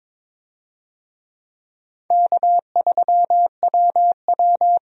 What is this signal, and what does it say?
Morse code, heard as a keyed tone, about 22 wpm, 700 Hz. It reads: K3WW
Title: USA Call Signs - 4 Characters - 22wpm